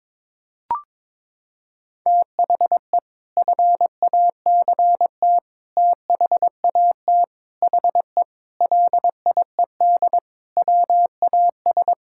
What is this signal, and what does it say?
Morse code, heard as a keyed tone, about 22 wpm, 700 Hz. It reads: THE FACT THAT HE LIED WAS
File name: practice / 22wpm / Sentences with 4-Word Phrases - 22wpm